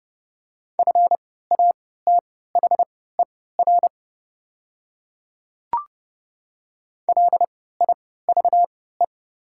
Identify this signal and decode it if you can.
Morse code, heard as a keyed tone, about 30 wpm, spaced as for 17 wpm, 700 Hz. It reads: FATHER LIVE
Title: 3x Character Spacing - Top 200 Words - 30wpm